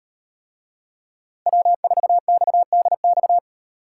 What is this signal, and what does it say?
Morse code, heard as a keyed tone, about 38 wpm, 700 Hz. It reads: W4XDX